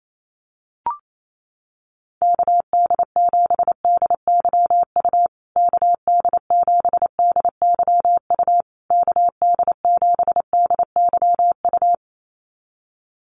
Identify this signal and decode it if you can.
Morse code, heard as a keyed tone, about 28 wpm, 700 Hz. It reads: KD7DYU KD7DYU KD7DYU
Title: USA Call Signs Repeated 3x 28wpm